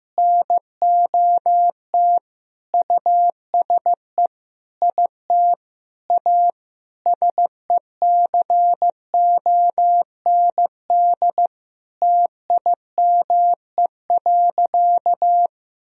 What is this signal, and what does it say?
Morse code, heard as a keyed tone, about 15 wpm, 700 Hz. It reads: NOT USE IT A SECOND TIME.
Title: Sentences from Top 1000 Words - Review Part 1 - 15wpm